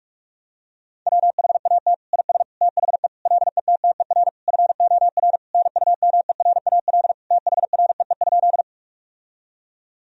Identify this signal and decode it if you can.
Morse code, heard as a keyed tone, about 43 wpm, 700 Hz. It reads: WHAT IS THE LETTER FOR NUMERAL THREE?
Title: Sentences from Top 1000 Words - Review Part 2 - 45wpm